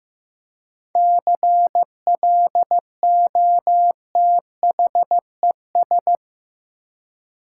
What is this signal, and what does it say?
Morse code, heard as a keyed tone, about 15 wpm, 700 Hz. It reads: CLOTHES
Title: Seven Letter Words - 15wpm